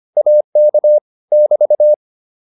Morse code, heard as a keyed tone, about 25 wpm, 600 Hz. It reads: AK <BT>